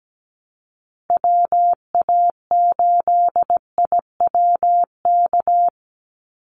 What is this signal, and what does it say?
Morse code, heard as a keyed tone, about 17 wpm, 700 Hz. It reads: WA8IWK
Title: Call Signs Repeated 2x - 17wpm